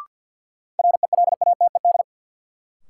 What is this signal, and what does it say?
Morse code, heard as a keyed tone, about 53 wpm, 700 Hz. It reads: RELATED
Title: Sentences from Top 1000 Words - Review Part 5 - New Word Drill - 50wpm